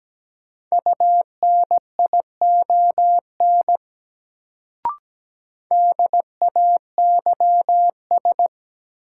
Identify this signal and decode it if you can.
Morse code, heard as a keyed tone, about 17 wpm, 700 Hz. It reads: UNION DAYS
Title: Sets of 5 Words Encore - New Word Drill - 17wpm